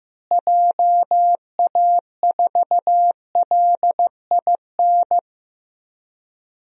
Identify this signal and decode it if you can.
Morse code, heard as a keyed tone, about 15 wpm, 700 Hz. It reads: JA4LIN
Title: DXCC-100 Call Signs - 6 Characters - Repeated 2x - 15wpm